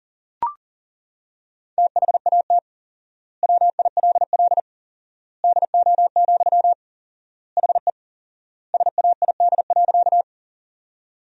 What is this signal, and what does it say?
Morse code, heard as a keyed tone, about 37 wpm, 700 Hz. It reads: THAT WILL DO, HE SAID.
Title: 4x Word-Spacing - Sentences from Top 100 Words - 40wpm